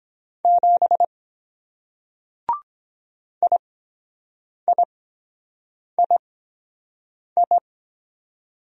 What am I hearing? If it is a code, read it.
Morse code, heard as a keyed tone, about 23 wpm, spaced as for 7 wpm, 700 Hz. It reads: 7 IIII